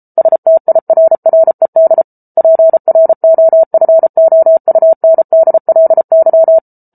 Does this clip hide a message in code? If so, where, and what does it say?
Yes: Morse code from 0.2 to 6.6 s, heard as a keyed tone, about 34 wpm, 650 Hz. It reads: STIRRED PROFOUNDLY